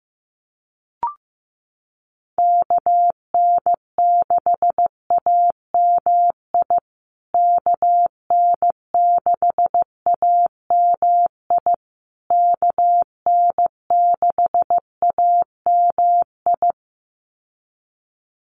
Morse code, heard as a keyed tone, about 15 wpm, 700 Hz. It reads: KN6AMI KN6AMI KN6AMI
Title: USA Call Signs Repeated 3x 15wpm